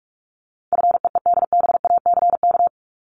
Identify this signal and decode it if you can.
Morse code, heard as a keyed tone, about 44 wpm, 700 Hz. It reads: FEEDBACK
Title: Top 1501-1750 Words 45wpm